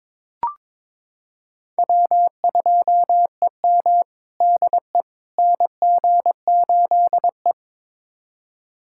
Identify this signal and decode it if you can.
Morse code, heard as a keyed tone, about 22 wpm, 700 Hz. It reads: W2EM DE NG8E
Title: USA Callsign DE Callsign - 4 Characters - 22wpm